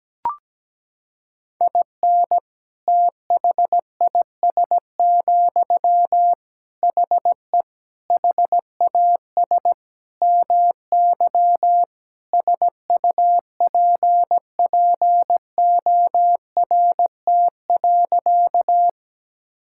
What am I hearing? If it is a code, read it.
Morse code, heard as a keyed tone, about 17 wpm, 700 Hz. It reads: IN THIS, HE HAS MY SUPPORT.
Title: Sentences from Top 1000 Words - Review Part 5 - 17wpm